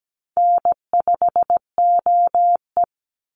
Morse code, heard as a keyed tone, about 17 wpm, 700 Hz. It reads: N5OE
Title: Easy Call Signs - 17wpm